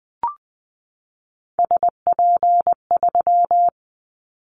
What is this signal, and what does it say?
Morse code, heard as a keyed tone, about 20 wpm, 700 Hz. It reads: SP3
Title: Callsign Prefixes for Top 25 DXCC - 20wpm